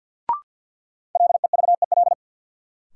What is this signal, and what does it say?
Morse code, heard as a keyed tone, about 50 wpm, 700 Hz. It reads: LEVEL
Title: Sentences from Top 700 Words - New Word Drill - 50wpm